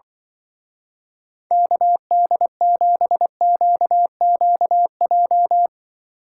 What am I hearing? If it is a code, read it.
Morse code, heard as a keyed tone, about 23 wpm, 700 Hz. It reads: KD7QQJ
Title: Speed Racing - USA Call Signs 20wpm